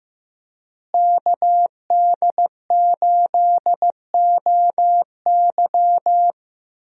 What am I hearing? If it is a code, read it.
Morse code, heard as a keyed tone, about 15 wpm, 700 Hz. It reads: KD8OY
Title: USA Call Signs - 5 Characters - Repeated 3x - 15wpm